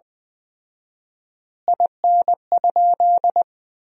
Morse code, heard as a keyed tone, about 20 wpm, 700 Hz. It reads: IN?
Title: 4x Word-Spacing - Sentences from Top 300 Words - 20wpm